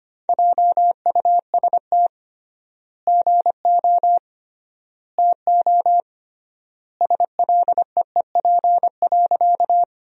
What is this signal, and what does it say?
Morse code, heard as a keyed tone, about 25 wpm, 700 Hz. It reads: JUST GO TO SLEEP.
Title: 3x Word-Spacing - Sentences from Top 500 Words - 25wpm